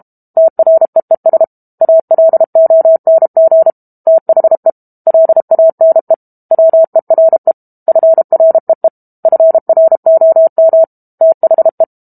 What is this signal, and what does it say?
Morse code, heard as a keyed tone, about 32 wpm, 650 Hz. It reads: TREES ALONG THE LANE WERE FREE FROM THE
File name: Book Two / 06-Chapter 6 - 32 WPM